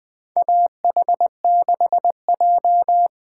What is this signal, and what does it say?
Morse code, heard as a keyed tone, about 20 wpm, 700 Hz. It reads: AH6J
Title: USA Call Signs - 8x Character Spacing - 20wpm